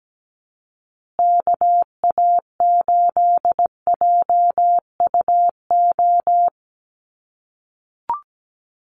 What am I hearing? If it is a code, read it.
Morse code, heard as a keyed tone, about 17 wpm, 700 Hz. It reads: KA8JUO
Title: USA Call Signs Repeated 2x 17wpm